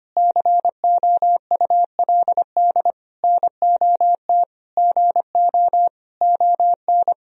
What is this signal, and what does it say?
Morse code, heard as a keyed tone, about 25 wpm, 700 Hz. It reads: COULD NOT GO ON
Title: Sentences from Top 1000 Words - Review Part 4 - 25wpm